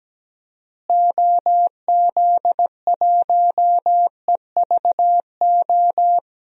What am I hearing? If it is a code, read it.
Morse code, heard as a keyed tone, about 17 wpm, 700 Hz. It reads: OZ1EVO
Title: DXCC-100 Call Signs - 6 Characters - Repeated 2x - 17wpm